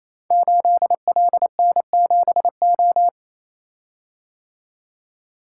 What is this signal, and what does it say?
Morse code, heard as a keyed tone, about 29 wpm, 700 Hz. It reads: 8LN7O
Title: Code Groups - 5 Letter:Numbers - 28wpm